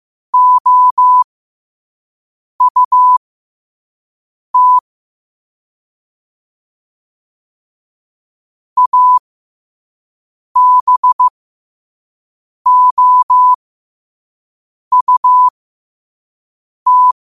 Morse code, heard as a keyed tone, about 15 wpm, spaced as for 5 wpm, 1000 Hz. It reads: OUT ABOUT
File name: qotdfiles / 5wpm